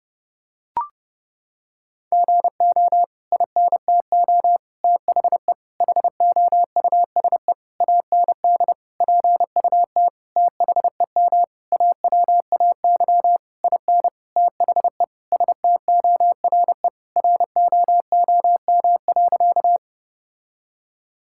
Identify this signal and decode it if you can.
Morse code, heard as a keyed tone, about 32 wpm, 700 Hz. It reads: GO INTO THE HOUSE AND PUT THEM AWAY IN THE STORE ROOM.
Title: Sentences from Top 1000 Words - Review Part 5 - 30wpm